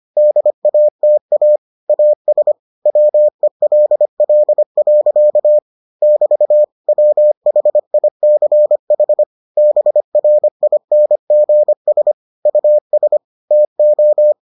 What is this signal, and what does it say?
Morse code, heard as a keyed tone, about 25 wpm, 600 Hz. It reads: DATA AS WELL. <BT> WHICH BRINGS US TO